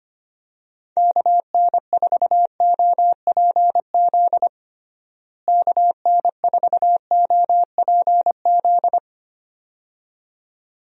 Morse code, heard as a keyed tone, about 25 wpm, 700 Hz. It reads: KN4OPZ KN4OPZ
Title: USA Call Signs - 6 Characters - Repeated 2x - 25wpm